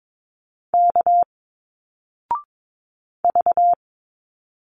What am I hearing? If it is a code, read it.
Morse code, heard as a keyed tone, about 22 wpm, 700 Hz. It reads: K V
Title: Single Letters 22wpm